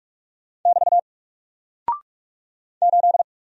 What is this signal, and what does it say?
Morse code, heard as a keyed tone, about 44 wpm, 700 Hz. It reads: <BT> 8
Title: Single Character (Letter:Number:Punctuation:Prosign) Alt 45wpm